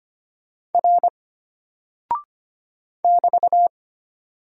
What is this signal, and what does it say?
Morse code, heard as a keyed tone, about 25 wpm, 700 Hz. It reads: R <BT>
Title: Single Character (Letter:Number:Punctuation:Prosign) Alt 25wpm